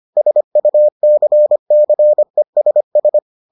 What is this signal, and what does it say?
Morse code, heard as a keyed tone, about 25 wpm, 600 Hz. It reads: SUCCESS